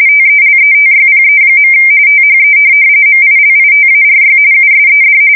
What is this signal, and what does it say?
RTTY, about 45.45 baud. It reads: TELETYPE. THE MILITARY USED
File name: fsk170 / teletype-RTTY